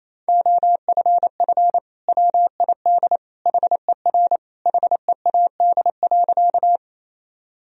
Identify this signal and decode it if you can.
Morse code, heard as a keyed tone, about 28 wpm, 700 Hz. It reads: OFF WID HER HEAD.
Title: Sentences from Top 300 Words and Abbreviations 28wpm